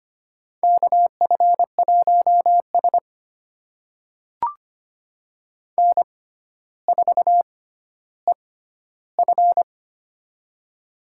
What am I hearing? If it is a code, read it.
Morse code, heard as a keyed tone, about 25 wpm, spaced as for 13 wpm, 700 Hz. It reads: KF1S N4EF
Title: USA Call Signs - 4 Characters - 6x Character Spacing - 25wpm